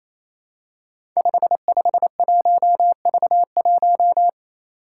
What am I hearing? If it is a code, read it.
Morse code, heard as a keyed tone, about 28 wpm, 700 Hz. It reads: 551V1
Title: Five Challenging Letter-Number Combinations 28wpm